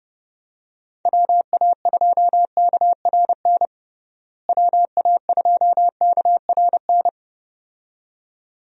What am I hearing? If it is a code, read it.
Morse code, heard as a keyed tone, about 30 wpm, 700 Hz. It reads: WA2KRN WA2KRN
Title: USA Call Signs - 6 Characters - Repeated 2x - 30wpm